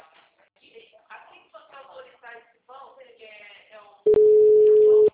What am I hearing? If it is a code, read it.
Ringback tone (Europe, 425 Hz), with speech.